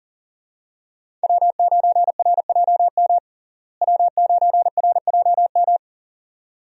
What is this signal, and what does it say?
Morse code, heard as a keyed tone, about 40 wpm, 700 Hz. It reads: W9RJM W9RJM
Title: USA Call Signs - 5 Characters - Repeated 2x - 40wpm